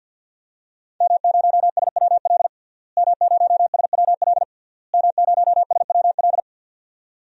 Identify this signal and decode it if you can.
Morse code, heard as a keyed tone, about 52 wpm, 700 Hz. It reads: M0SWL M0SWL M0SWL
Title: DXCC-100 Call Signs - 5 Characters - Repeated 3x - 50wpm